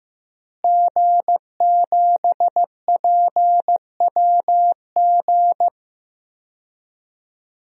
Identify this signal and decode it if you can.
Morse code, heard as a keyed tone, about 15 wpm, 700 Hz. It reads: G7PWG
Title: DXCC-100 Call Signs - 5 Characters - Repeated 3x - 15wpm